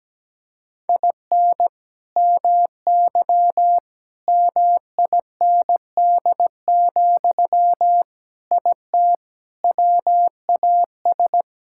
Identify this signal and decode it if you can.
Morse code, heard as a keyed tone, about 17 wpm, 700 Hz. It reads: IN MY MIND, IT WAS
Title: Sentences from Top 1000 Words - Review Part 6 - 17wpm